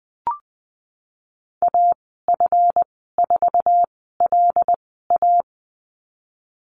Morse code, heard as a keyed tone, about 20 wpm, spaced as for 15 wpm, 700 Hz. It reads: AF4LA
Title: 173 Lesson 28g - Number 9 - Call Signs 20wpm